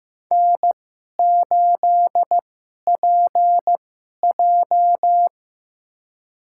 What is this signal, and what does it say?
Morse code, heard as a keyed tone, about 15 wpm, spaced as for 11 wpm, 700 Hz. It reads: N8PJ
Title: USA Call Signs - 4 Characters - 2x Character Spacing - 15wpm